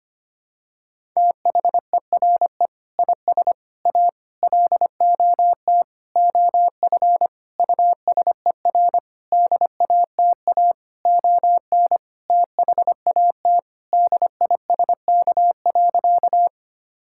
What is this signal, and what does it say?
Morse code, heard as a keyed tone, about 25 wpm, 700 Hz. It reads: THERE IS A LOT OF USER DATA ON THAT DISK.
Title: Sentences from Top 1000 Words - Review Part 6 - 25wpm